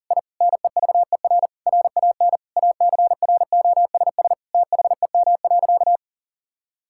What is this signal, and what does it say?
Morse code, heard as a keyed tone, about 40 wpm, 700 Hz. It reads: I NEVER RAN ACROSS THEM.
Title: Sentences from Top 1000 Words - Review Part 5 - 40wpm